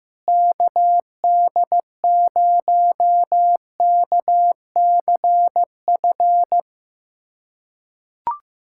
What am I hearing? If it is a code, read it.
Morse code, heard as a keyed tone, about 15 wpm, 700 Hz. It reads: KD0KCF
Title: USA Callsign DE Callsign - 6 Characters - 15wpm